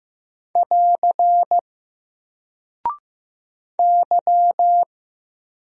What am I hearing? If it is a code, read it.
Morse code, heard as a keyed tone, about 15 wpm, 700 Hz. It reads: <AR> Y
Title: Single Character (Letter:Number:Punctuation:Prosign) 15wpm